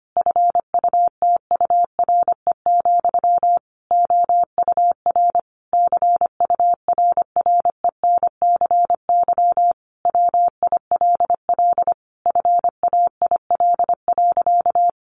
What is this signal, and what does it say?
Morse code, heard as a keyed tone, about 25 wpm, 700 Hz. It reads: FUTURE, OUR CURRENCY WILL FAIL.
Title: Sentences from Top 1500 Words 25wpm